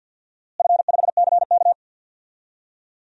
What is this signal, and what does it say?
Morse code, heard as a keyed tone, about 52 wpm, 700 Hz. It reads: F5CK
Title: DXCC-100 Call Signs - 4 Characters - Repeated 2x - 50wpm